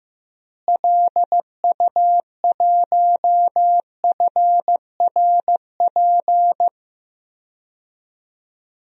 Morse code, heard as a keyed tone, about 15 wpm, 700 Hz. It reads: LU1FRP
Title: DXCC-100 Call Signs - 6 Characters - Repeated 2x - 15wpm